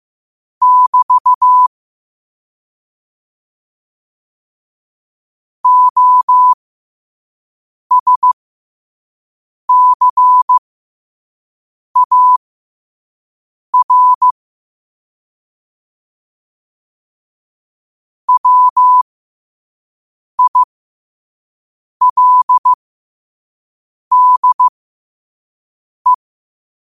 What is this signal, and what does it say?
Morse code, heard as a keyed tone, about 15 wpm, spaced as for 5 wpm, 1000 Hz. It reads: <BT> OSCAR WILDE
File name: qotdfiles / 5wpm